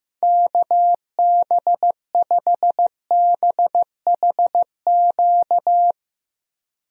Morse code, heard as a keyed tone, about 15 wpm, 700 Hz. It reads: KB5BHQ
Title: USA Call Signs - 6 Characters - Repeated 3x - 15wpm